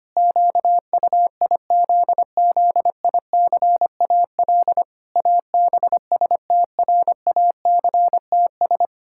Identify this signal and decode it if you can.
Morse code, heard as a keyed tone, about 25 wpm, 700 Hz. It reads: QUIZZICAL ABSTRACTS
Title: Pangram Challenge v2 - 25wpm